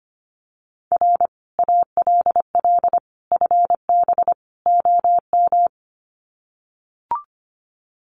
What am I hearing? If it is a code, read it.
Morse code, heard as a keyed tone, about 25 wpm, 700 Hz. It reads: R ALL FB OM
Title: Common QSO Phrases 25wpm